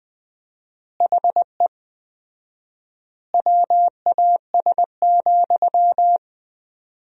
Morse code, heard as a keyed tone, about 21 wpm, 700 Hz. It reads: HE WAS,
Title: 4x Word-Spacing - Sentences from Top 400 Words - 20wpm